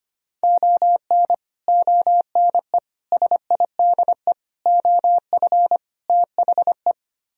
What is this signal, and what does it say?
Morse code, heard as a keyed tone, about 33 wpm, 700 Hz. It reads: ON ONE SIDE OF THE